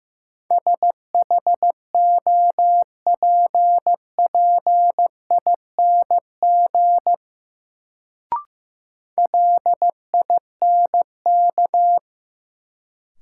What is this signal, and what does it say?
Morse code, heard as a keyed tone, about 15 wpm, 700 Hz. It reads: SHOPPING LINK
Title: Sentences from Top 1000 Words - Review Part 7 - New Word Drill - 15wpm